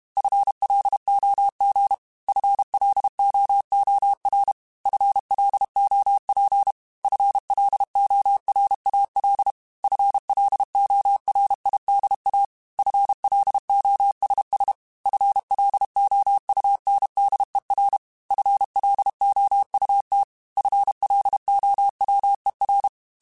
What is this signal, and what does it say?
Morse code, heard as a keyed tone, about 32 wpm, 800 Hz. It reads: FLOG FLOOR FLOP FLORAL FLORIDA FLOSS FLOUNDER FLOUT FLOWER
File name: FL Runs / FL Runs mp3s / FL Runs @30